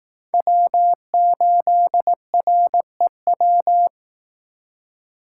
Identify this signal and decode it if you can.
Morse code, heard as a keyed tone, about 18 wpm, 700 Hz. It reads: W8REW
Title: Speed Racing - USA 1x3 Call Signs 15wpm